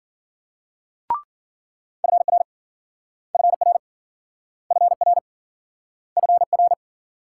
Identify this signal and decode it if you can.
Morse code, heard as a keyed tone, about 54 wpm, 700 Hz. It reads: FR FR FR FR